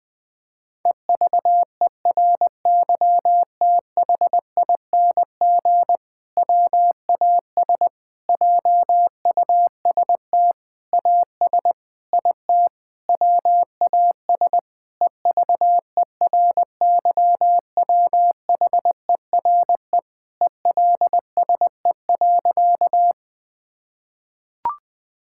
Morse code, heard as a keyed tone, about 20 wpm, 700 Hz. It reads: EVERYTHING WAS JUST AS IT WAS EVERYWHERE ELSE.
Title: Sentences from Top 1600 Words 20wpm